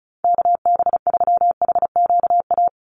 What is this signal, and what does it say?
Morse code, heard as a keyed tone, about 34 wpm, 700 Hz. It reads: KB3HQA
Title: USA Call Signs 35wpm